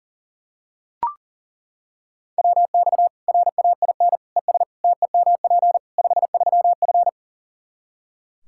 Morse code, heard as a keyed tone, about 40 wpm, 700 Hz. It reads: WX RAIN ES TEMP 53F
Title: Practice QSO - Exchange 2 - Info - 40wpm